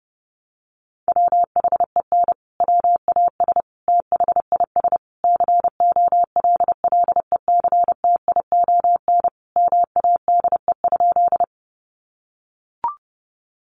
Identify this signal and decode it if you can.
Morse code, heard as a keyed tone, about 30 wpm, 700 Hz. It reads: WHEN WAS THIS COLLECTION MADE?
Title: Sentences from Top 1200 Words 30wpm